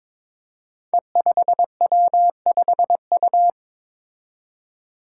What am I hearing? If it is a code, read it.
Morse code, heard as a keyed tone, about 22 wpm, 700 Hz. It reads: E5W5U
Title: Five Challenging Letter-Number Combinations 22wpm